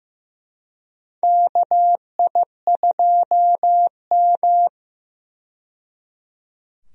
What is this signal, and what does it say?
Morse code, heard as a keyed tone, about 15 wpm, 700 Hz. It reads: KI2M
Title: USA Call Signs - 4 Characters - 15wpm